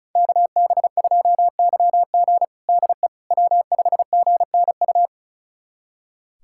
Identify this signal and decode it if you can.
Morse code, heard as a keyed tone, about 36 wpm, 700 Hz. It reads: KB2YG DE W5GNU
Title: USA Callsign DE Callsign - 5 Characters - 35wpm